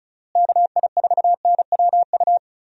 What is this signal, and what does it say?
Morse code, heard as a keyed tone, about 34 wpm, 700 Hz. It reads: KI4NWU
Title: USA Call Signs - 6 Characters - Repeated 3x - 35wpm